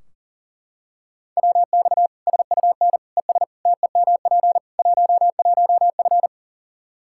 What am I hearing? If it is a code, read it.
Morse code, heard as a keyed tone, about 40 wpm, 700 Hz. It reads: WX SUN ES TEMP 11F
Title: Practice QSO - Exchange 2 - Info - 40wpm